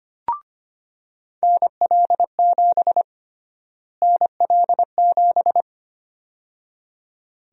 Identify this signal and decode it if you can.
Morse code, heard as a keyed tone, about 25 wpm, 700 Hz. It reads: NL7 NL7
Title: USA Call Sign Prefixes - Repeated 2x - 25wpm